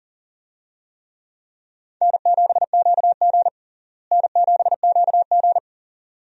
Morse code, heard as a keyed tone, about 40 wpm, 700 Hz. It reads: N7QG N7QG
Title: USA Call Signs - 4 Characters - Repeated 2x - 40wpm